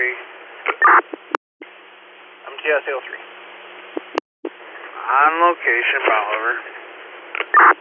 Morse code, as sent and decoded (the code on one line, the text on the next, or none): none